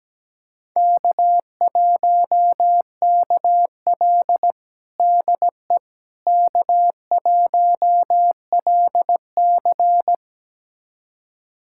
Morse code, heard as a keyed tone, about 17 wpm, 700 Hz. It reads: K1KL DE K1LC
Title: USA Callsign DE Callsign - 4 Characters - 17wpm